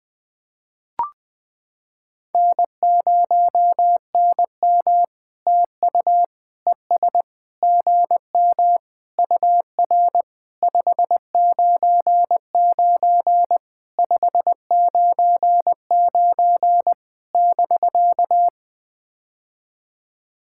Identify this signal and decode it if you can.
Morse code, heard as a keyed tone, about 20 wpm, 700 Hz. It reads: N0NM TU ES GM UR 599 599 <BK>
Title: POTA QSO Practice - 20wpm